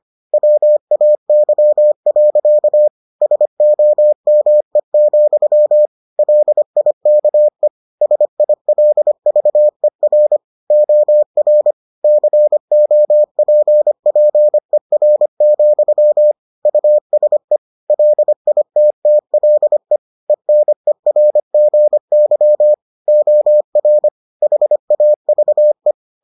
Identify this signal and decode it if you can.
Morse code, heard as a keyed tone, about 25 wpm, 600 Hz. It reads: WAY. SOME, LIKE SILVER OR COPPER, USE LITTLE ENERGY OR HAVE